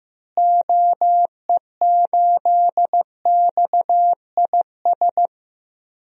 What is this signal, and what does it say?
Morse code, heard as a keyed tone, about 15 wpm, 700 Hz. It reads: OE8XIS
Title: DXCC-100 Call Signs - 6 Characters - Repeated 3x - 15wpm